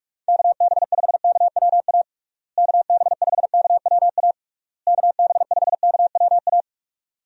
Morse code, heard as a keyed tone, about 47 wpm, 700 Hz. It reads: KB5KWA KB5KWA KB5KWA
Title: USA Call Signs - 6 Characters - Repeated 3x - 45wpm